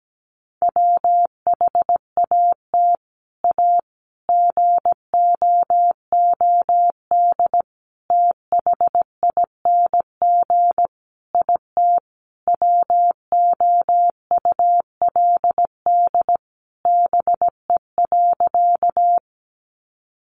Morse code, heard as a keyed tone, about 17 wpm, 700 Hz. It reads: WHAT A GOOD THING IT WOULD BE.
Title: Sentences from Top 200 Words 17wpm